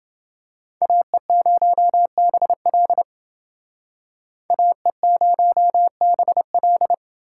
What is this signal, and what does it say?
Morse code, heard as a keyed tone, about 28 wpm, 700 Hz. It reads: AE0BL AE0BL